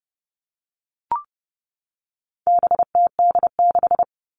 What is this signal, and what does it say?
Morse code, heard as a keyed tone, about 30 wpm, 700 Hz. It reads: BTD6
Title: All Cut Numbers - 4-Digit 30wpm